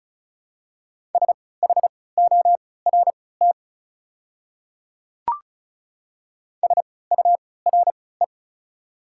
Morse code, heard as a keyed tone, about 34 wpm, spaced as for 20 wpm, 690 Hz. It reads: SHORT SURE
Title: 3x Character Spacing - Top 500 Words - 35wpm